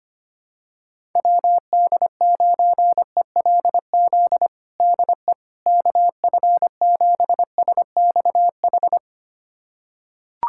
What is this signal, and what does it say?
Morse code, heard as a keyed tone, about 25 wpm, 700 Hz. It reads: WD9ELZ DE KF7SXH
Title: USA Callsign DE Callsign - 6 Characters - 25wpm